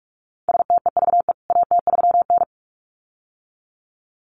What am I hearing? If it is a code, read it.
Morse code, heard as a keyed tone, about 44 wpm, 700 Hz. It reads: STEVE AT3N
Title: CWT Exchange 45wpm